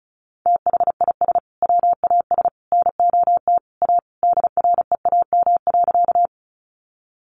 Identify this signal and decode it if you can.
Morse code, heard as a keyed tone, about 34 wpm, 700 Hz. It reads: THIS WAS NOT A DREAM.
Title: Sentences from Top 1500 Words 35wpm